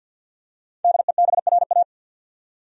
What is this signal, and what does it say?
Morse code, heard as a keyed tone, about 51 wpm, 700 Hz. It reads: DEBRA
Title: USA Common Names - Max 5 Letters - 50wpm